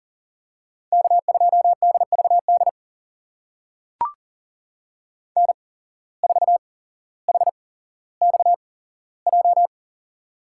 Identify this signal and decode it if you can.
Morse code, heard as a keyed tone, about 40 wpm, spaced as for 17 wpm, 700 Hz. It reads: K2DVD N4HXJ
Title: USA Call Signs - 5 Characters - 8x Character Spacing - 40wpm